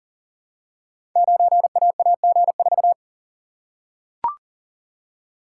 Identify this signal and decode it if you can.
Morse code, heard as a keyed tone, about 41 wpm, 700 Hz. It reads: 9AAG4
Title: Code Groups - 5 Letter:Numbers - 40wpm